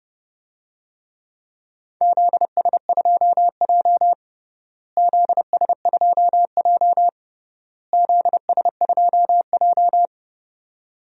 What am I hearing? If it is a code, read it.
Morse code, heard as a keyed tone, about 30 wpm, 700 Hz. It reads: ZS2J ZS2J ZS2J
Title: DXCC-100 Call Signs - 4 Characters - Repeated 3x - 30wpm